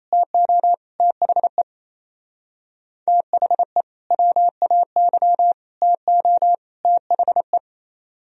Morse code, heard as a keyed tone, about 33 wpm, 700 Hz. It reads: TO THE THE WAY TO THE
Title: Speed-Racing - Sets of 4 Words 28wpm